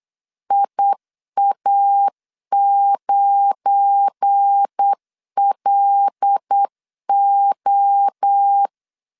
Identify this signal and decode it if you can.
Morse code, heard as a keyed tone, about 8 wpm, 800 Hz. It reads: IA9LO
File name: LEZIONI / 10 SIGNS / 10B INT 8wpm 3534